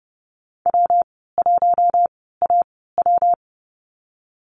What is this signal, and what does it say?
Morse code, heard as a keyed tone, about 30 wpm, spaced as for 17 wpm, 700 Hz. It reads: W1AW
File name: course / 30:10wpm / 116 Lesson 19g - Letter w - Call Signs 30wpm